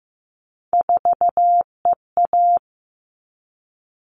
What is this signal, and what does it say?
Morse code, heard as a keyed tone, about 15 wpm, 700 Hz. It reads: 4EA
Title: All Cut Numbers - 3-Digit 15wpm